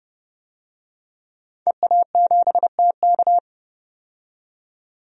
Morse code, heard as a keyed tone, about 32 wpm, 700 Hz. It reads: EA7TK
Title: DXCC-100 Call Signs - 5 Characters - 30wpm